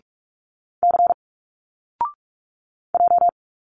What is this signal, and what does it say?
Morse code, heard as a keyed tone, about 46 wpm, 700 Hz. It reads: C J
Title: Single Letters 45wpm